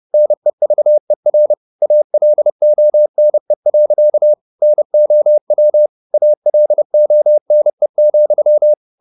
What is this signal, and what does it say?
Morse code, heard as a keyed tone, about 30 wpm, 600 Hz. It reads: NEVER ALONE. NOW ALONE,
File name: the old man and the sea / the old man and the sea 1 30wpm